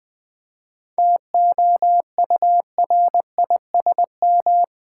Morse code, heard as a keyed tone, about 20 wpm, 700 Hz. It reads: TOURISM
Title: 7 Letter Words - First 3 Letters Given - 20wpm